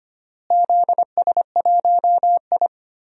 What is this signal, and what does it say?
Morse code, heard as a keyed tone, about 25 wpm, 700 Hz. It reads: ZS1I
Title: DXCC-100 Call Signs - 4 Characters - Repeated 2x - 25wpm